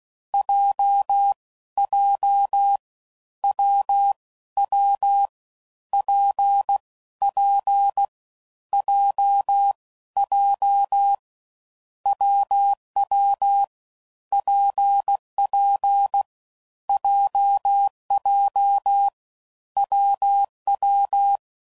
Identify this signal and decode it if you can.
Morse code, heard as a keyed tone, about 16 wpm, 800 Hz. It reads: J J W W P P J J WW PP JJ WW
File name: Combinations / Difficult Combos / WPJ WPJ1